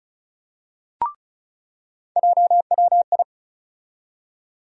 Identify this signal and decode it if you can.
Morse code, heard as a keyed tone, about 36 wpm, 700 Hz. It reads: JWI
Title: USA Call Sign Postfixes - 3 Letters 35wpm